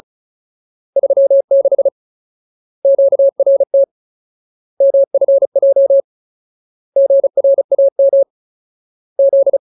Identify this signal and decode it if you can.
Morse code, heard as a keyed tone, about 34 wpm, 550 Hz. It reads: 36 QRT MFJ GRAM Z
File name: Ultimate Word List / 35WPM Ultimate